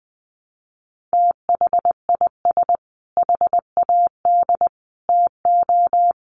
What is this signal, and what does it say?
Morse code, heard as a keyed tone, about 20 wpm, 700 Hz. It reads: THIS HAD TO
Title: Sentences from Top 1300 Words 20wpm